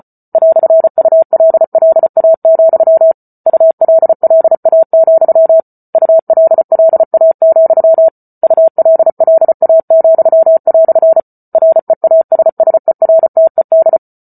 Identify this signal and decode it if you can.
Morse code, heard as a keyed tone, about 34 wpm, 650 Hz. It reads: "ULLA, ULLA, ULLA, ULLA," REASSERTED